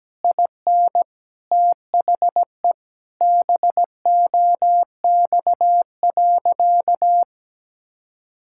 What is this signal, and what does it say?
Morse code, heard as a keyed tone, about 17 wpm, 700 Hz. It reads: IN THE BOX.
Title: Sentences from Top 1000 Words - Review Part 6 - 17wpm